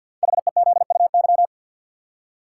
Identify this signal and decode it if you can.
Morse code, heard as a keyed tone, about 52 wpm, 700 Hz. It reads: HEZUY
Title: Code Groups - 5 Letters - 50wpm